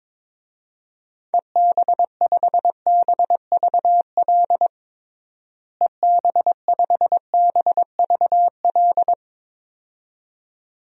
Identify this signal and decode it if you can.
Morse code, heard as a keyed tone, about 22 wpm, 700 Hz. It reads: EB5BVL EB5BVL
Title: DXCC-100 Call Signs - 6 Characters - Repeated 2x - 22wpm